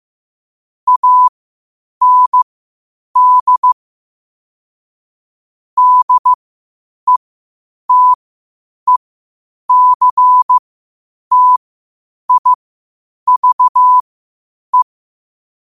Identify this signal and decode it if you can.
Morse code, heard as a keyed tone, about 15 wpm, spaced as for 8 wpm, 1000 Hz. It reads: AND DETECTIVE
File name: qotdfiles / 8wpm